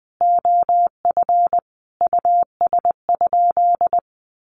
Morse code, heard as a keyed tone, about 20 wpm, 700 Hz. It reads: OF US?
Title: Sentences from Top 1100 Words 20wpm